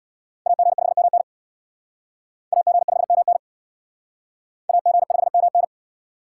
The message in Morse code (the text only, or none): AG5MN AG5MN AG5MN